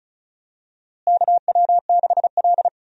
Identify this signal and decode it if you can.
Morse code, heard as a keyed tone, about 34 wpm, 700 Hz. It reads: KW6L
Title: USA Call Signs - 4 Characters - Repeated 2x - 35wpm